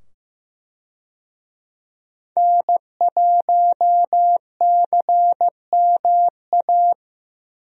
Morse code, heard as a keyed tone, about 15 wpm, 700 Hz. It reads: N1CMA
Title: USA Call Signs - 5 Characters - 15wpm